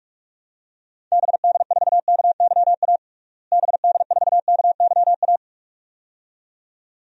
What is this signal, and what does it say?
Morse code, heard as a keyed tone, about 42 wpm, 700 Hz. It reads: BD4KYA BD4KYA
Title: DXCC-100 Call Signs - 6 Characters - Repeated 2x - 45wpm